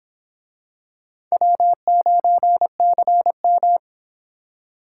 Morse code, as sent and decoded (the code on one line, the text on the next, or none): .-- ----. -.-. --
W9CM